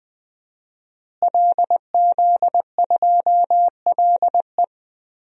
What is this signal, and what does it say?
Morse code, heard as a keyed tone, about 20 wpm, 700 Hz. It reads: LZ2LE
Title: DXCC-100 Call Signs - 5 Characters - Repeated 2x - 20wpm